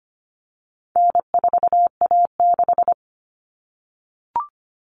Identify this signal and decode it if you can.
Morse code, heard as a keyed tone, about 25 wpm, 700 Hz. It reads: N4A6
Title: All Cut Numbers - 4-Digit 25wpm